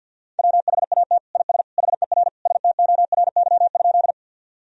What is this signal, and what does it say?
Morse code, heard as a keyed tone, about 52 wpm, 700 Hz. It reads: WHAT IS HER STORY?
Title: Sentences from Top 1000 Words - Review Part 1 - 50wpm